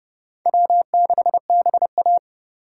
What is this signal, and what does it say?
Morse code, heard as a keyed tone, about 37 wpm, 700 Hz. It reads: W6BA
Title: Speed Racing - USA Call Signs 30wpm